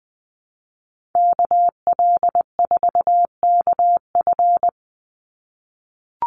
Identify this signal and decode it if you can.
Morse code, heard as a keyed tone, about 20 wpm, 700 Hz. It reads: KL4KF
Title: USA Call Signs 20wpm